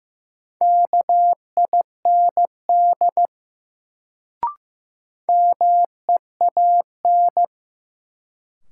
Morse code, heard as a keyed tone, about 15 wpm, 700 Hz. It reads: KIND MEAN
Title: Sentences from Top 200 Words - New Word Drill - 15wpm